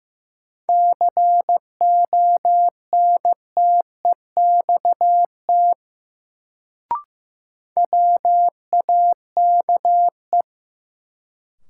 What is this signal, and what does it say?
Morse code, heard as a keyed tone, about 15 wpm, 700 Hz. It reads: CONTEXT WAKE
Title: Sets of 4 Words - New Word Drill - 15wpm